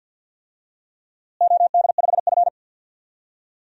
Morse code, heard as a keyed tone, about 51 wpm, 700 Hz. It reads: OD5F
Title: DXCC-100 Call Signs - 4 Characters - Repeated 3x - 50wpm